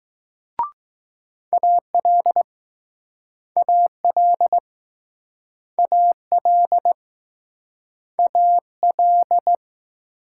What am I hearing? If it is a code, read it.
Morse code, heard as a keyed tone, about 20 wpm, 700 Hz. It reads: AL AL AL AL